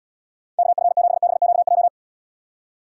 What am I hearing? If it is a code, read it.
Morse code, heard as a keyed tone, about 73 wpm, 700 Hz. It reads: KF7BQJ